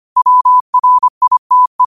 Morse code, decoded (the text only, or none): WRITE